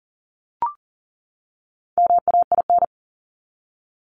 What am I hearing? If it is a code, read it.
Morse code, heard as a keyed tone, about 40 wpm, 700 Hz. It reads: MAIN
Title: Top 300 Words 40wpm